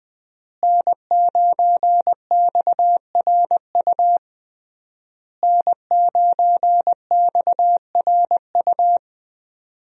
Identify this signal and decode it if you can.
Morse code, heard as a keyed tone, about 20 wpm, 700 Hz. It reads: N9XRU N9XRU
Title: USA Call Signs - 5 Characters - Repeated 3x - 20wpm